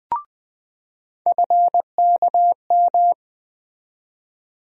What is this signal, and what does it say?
Morse code, heard as a keyed tone, about 20 wpm, 700 Hz. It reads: FKM
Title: USA Call Sign Postfixes - 3 Letters 20wpm